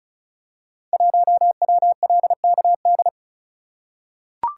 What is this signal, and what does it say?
Morse code, heard as a keyed tone, about 36 wpm, 690 Hz. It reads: 1WLKD
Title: Code Groups - 5 Letter:Numbers - 35wpm